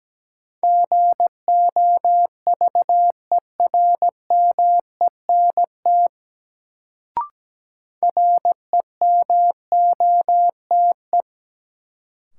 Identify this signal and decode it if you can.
Morse code, heard as a keyed tone, about 17 wpm, 700 Hz. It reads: GOVERMENT REMOTE
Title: Sentences from Top 1300 Words - New Word Drill - 17wpm